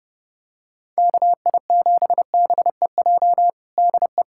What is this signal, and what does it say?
Morse code, heard as a keyed tone, about 30 wpm, 700 Hz. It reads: KI7BEJ DE
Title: USA Callsign DE Callsign - 6 Characters - 30wpm